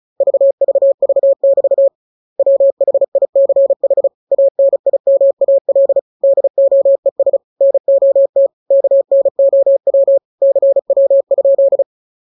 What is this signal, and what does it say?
Morse code, heard as a keyed tone, about 34 wpm, 550 Hz. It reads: VVV<BT> WHICH ANIMAL DOES NOT KNOW CW?
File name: jokes / joke 26 35 0000